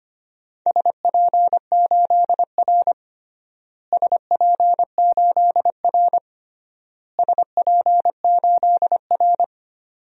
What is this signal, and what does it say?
Morse code, heard as a keyed tone, about 25 wpm, 700 Hz. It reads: SP8R SP8R SP8R
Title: DXCC-100 Call Signs - 4 Characters - Repeated 3x - 25wpm